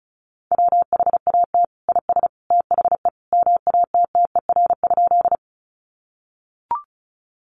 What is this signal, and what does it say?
Morse code, heard as a keyed tone, about 34 wpm, 700 Hz. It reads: WHAT IS THE MATTER?
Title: Sentences from Top 700 Words 35wpm